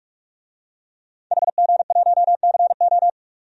Morse code, heard as a keyed tone, about 45 wpm, 700 Hz. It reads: HG1CO
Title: DXCC-100 Call Signs - 5 Characters - 45wpm